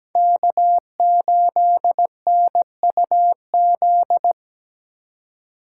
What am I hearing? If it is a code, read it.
Morse code, heard as a keyed tone, about 20 wpm, 700 Hz. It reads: K8NUZ
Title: Speed Racing - USA 1x3 Call Signs 17wpm